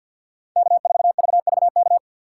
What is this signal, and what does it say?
Morse code, heard as a keyed tone, about 48 wpm, 700 Hz. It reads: K4VVK
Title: USA Call Signs - 5 Characters - 4x Character Spacing - 50wpm